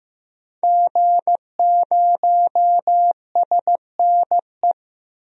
Morse code, heard as a keyed tone, about 15 wpm, 700 Hz. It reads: G0SNE
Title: DXCC-100 Call Signs - 5 Characters - 15wpm